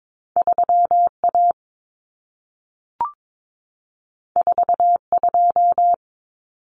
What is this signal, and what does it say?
Morse code, heard as a keyed tone, about 22 wpm, 700 Hz. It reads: 3A 42
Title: Common Cut Numbers - 2-Digit 22wpm